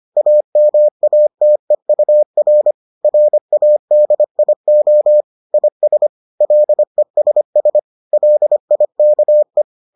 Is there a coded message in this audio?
Yes: Morse code from 0.2 to 9.6 s, heard as a keyed tone, about 25 wpm, 600 Hz. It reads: AMATEUR RADIO IS LESS LIKE